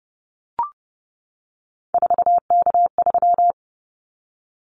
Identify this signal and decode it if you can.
Morse code, heard as a keyed tone, about 31 wpm, 700 Hz. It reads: 4K3
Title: Callsign Prefixes for Top 100 DXCC - 30wpm